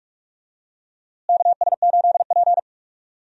Morse code, heard as a keyed tone, about 46 wpm, 700 Hz. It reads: KS8P
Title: USA Call Signs - 4 Characters - Repeated 3x - 45wpm